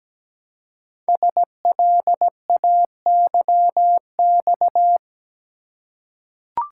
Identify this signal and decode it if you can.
Morse code, heard as a keyed tone, about 17 wpm, 700 Hz. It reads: SLAYX
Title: Code Groups - 5 Letters - 17wpm